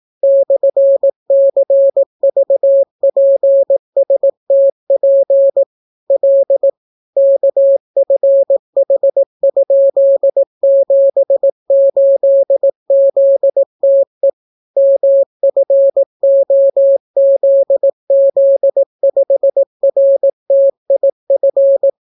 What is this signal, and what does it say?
Morse code, heard as a keyed tone, about 18 wpm, 550 Hz. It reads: /CVPSTP L KFH?78ZTE MFOZZ5RTIF